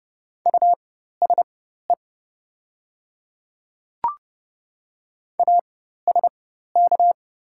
Morse code, heard as a keyed tone, about 30 wpm, spaced as for 14 wpm, 700 Hz. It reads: USE ASK
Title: 4x Character Spacing - Top 200 Words - 30wpm